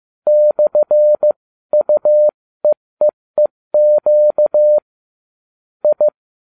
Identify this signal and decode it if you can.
Morse code, heard as a keyed tone, about 15 wpm, spaced as for 13 wpm, 600 Hz. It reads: /UEEEQ I